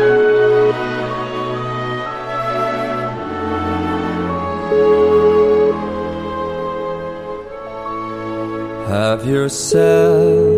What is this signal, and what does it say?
Ringback tone (Europe, 425 Hz), with noise.